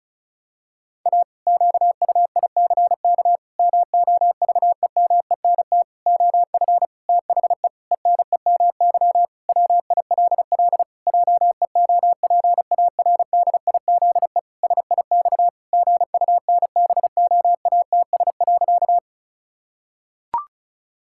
Morse code, heard as a keyed tone, about 34 wpm, 700 Hz. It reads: A QUICK MOVEMENT OF THE ENEMY WILL JEOPARDIZE SIX GUNBOATS.
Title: Pangram Challenge v2 - 35wpm